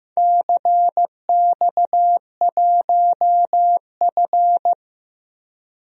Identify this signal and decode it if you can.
Morse code, heard as a keyed tone, about 15 wpm, 700 Hz. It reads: CX1F
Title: DXCC-100 Call Signs - 4 Characters - 15wpm